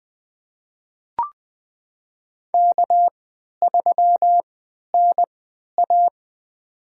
Morse code, heard as a keyed tone, about 20 wpm, spaced as for 11 wpm, 700 Hz. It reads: K3NA
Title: USA Call Signs - 4 Characters - 3x Character Spacing - 20wpm